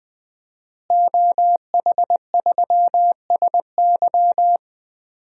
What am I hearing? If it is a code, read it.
Morse code, heard as a keyed tone, about 20 wpm, 700 Hz. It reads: OH3SY
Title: DXCC-100 Call Signs - 5 Characters - Repeated 2x - 20wpm